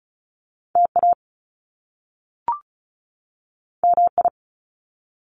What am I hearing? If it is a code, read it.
Morse code, heard as a keyed tone, about 36 wpm, 700 Hz. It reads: TA MI
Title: Top Two Letter Combinations in Words 35wpm